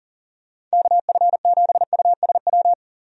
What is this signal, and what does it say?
Morse code, heard as a keyed tone, about 40 wpm, 700 Hz. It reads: KF7USW
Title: USA Call Signs - 6 Characters - 40wpm